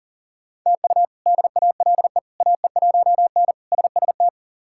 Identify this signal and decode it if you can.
Morse code, heard as a keyed tone, about 40 wpm, 700 Hz. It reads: TU DALE AE1N SST
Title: K1USN SST QSO Practice - 40wpm